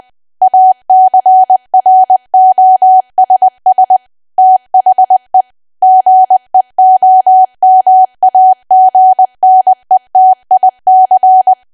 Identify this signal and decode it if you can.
Morse code, heard as a keyed tone, about 20 wpm, 750 Hz. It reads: ACROSS THE GEOMAGNETIC
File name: cwa / W1AW-Level3 / 20WPM